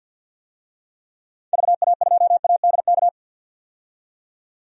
Morse code, heard as a keyed tone, about 46 wpm, 700 Hz. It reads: VA2ADK
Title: DXCC-100 Call Signs - 6 Characters - 50wpm